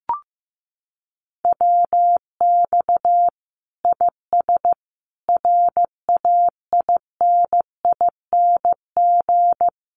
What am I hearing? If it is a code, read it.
Morse code, heard as a keyed tone, about 15 wpm, 700 Hz. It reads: WX IS RAINING
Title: Common QSO Phrases 15wpm